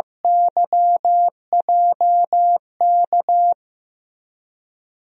Morse code, heard as a keyed tone, about 15 wpm, 700 Hz. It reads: YJK
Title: USA Call Signs - 6 Characters - Repeated 2x - 15wpm